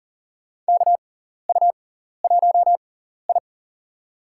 Morse code, heard as a keyed tone, about 38 wpm, spaced as for 14 wpm, 690 Hz. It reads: KU1I
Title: USA Call Signs - 6x Character Spacing - 40wpm